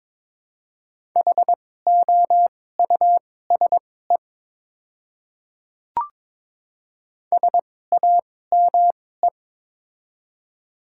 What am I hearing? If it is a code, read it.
Morse code, heard as a keyed tone, about 22 wpm, spaced as for 16 wpm, 700 Hz. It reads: HOUSE SAME
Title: 2x Character Spacing - Top 200 Words - 22wpm